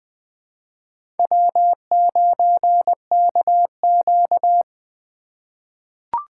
Morse code, heard as a keyed tone, about 20 wpm, 700 Hz. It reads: W9KQ
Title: USA Call Signs - 4 Characters - 4x Character Spacing - 20wpm